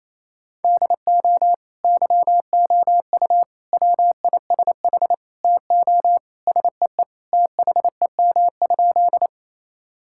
Morse code, heard as a keyed tone, about 28 wpm, 700 Hz. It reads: DO YOU WISH TO SEE THEM?
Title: Sentences from Top 1600 Words 28wpm